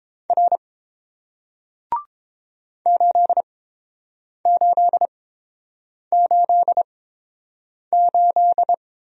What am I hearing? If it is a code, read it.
Morse code, heard as a keyed tone, about 30 wpm, spaced as for 7 wpm, 690 Hz. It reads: R 8888